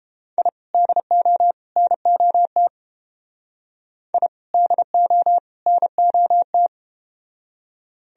Morse code, heard as a keyed tone, about 31 wpm, 700 Hz. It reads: I DO NOT I DO NOT